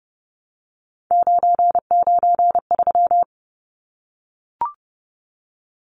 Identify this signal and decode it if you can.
Morse code, heard as a keyed tone, about 30 wpm, 700 Hz. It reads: 993
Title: Numbers - 3-Digit 30wpm